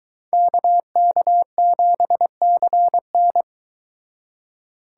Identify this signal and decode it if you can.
Morse code, heard as a keyed tone, about 22 wpm, 700 Hz. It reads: KK7CN